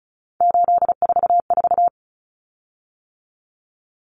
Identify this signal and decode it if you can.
Morse code, heard as a keyed tone, about 34 wpm, 700 Hz. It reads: 844
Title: Common Cut Numbers - 3-Digit 35wpm